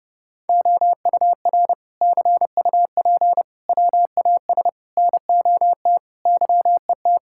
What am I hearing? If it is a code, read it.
Morse code, heard as a keyed tone, about 30 wpm, 700 Hz. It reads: OUR CUP WAS NOT YET
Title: Sentences from Top 1600 Words 30wpm